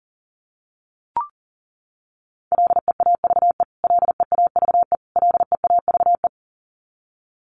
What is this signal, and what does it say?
Morse code, heard as a keyed tone, about 40 wpm, 700 Hz. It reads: LEAVE LEAVE LEAVE
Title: Top 500 Words Repeated 40wpm